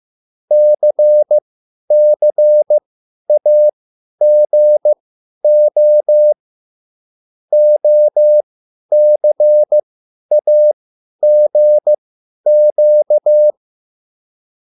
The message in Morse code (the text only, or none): CCAGO OCAGQ